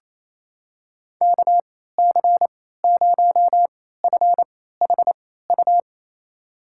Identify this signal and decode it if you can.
Morse code, heard as a keyed tone, about 28 wpm, spaced as for 16 wpm, 700 Hz. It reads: KC0FHU
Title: USA Call Signs - 6 Characters - 3x Character Spacing - 28wpm